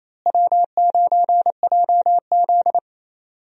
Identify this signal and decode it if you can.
Morse code, heard as a keyed tone, about 28 wpm, 700 Hz. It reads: W9JZ
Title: USA Call Signs - 4 Characters - Repeated 2x - 28wpm